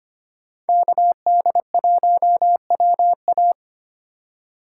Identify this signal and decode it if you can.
Morse code, heard as a keyed tone, about 25 wpm, 700 Hz. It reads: KD1WA
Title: USA Call Signs - 5 Characters - 2x Character Spacing - 25wpm